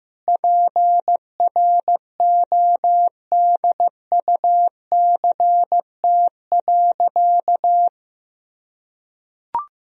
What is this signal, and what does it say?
Morse code, heard as a keyed tone, about 15 wpm, 700 Hz. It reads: PRODUCT.
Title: Sentences from Top 1000 Words - Review Part 2 - 15wpm